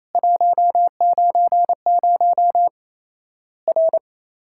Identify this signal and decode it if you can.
Morse code, heard as a keyed tone, about 28 wpm, 700 Hz. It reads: 190 R
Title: Worked All Europe DX Contest - QTC Exchange - 28wpm